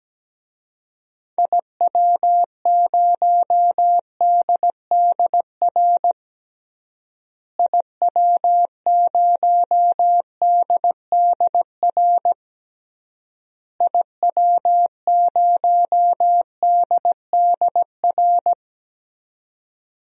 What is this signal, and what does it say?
Morse code, heard as a keyed tone, about 17 wpm, 700 Hz. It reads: IW0DDR IW0DDR IW0DDR
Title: DXCC-100 Call Signs - 6 Characters - Repeated 3x - 17wpm